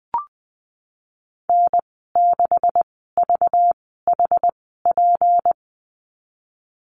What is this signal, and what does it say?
Morse code, heard as a keyed tone, about 20 wpm, spaced as for 15 wpm, 700 Hz. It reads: N6VHP
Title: 193 Lesson 31g - Letter v - Call Signs 20wpm